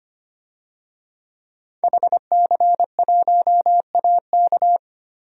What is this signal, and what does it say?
Morse code, heard as a keyed tone, about 25 wpm, 700 Hz. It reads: HC1AK
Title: DXCC-100 Call Signs - 5 Characters - Repeated 2x - 25wpm